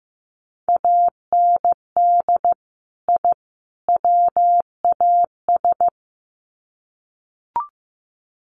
Morse code, heard as a keyed tone, about 15 wpm, 700 Hz. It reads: AND I WAS
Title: Sets of 3 Words 15wpm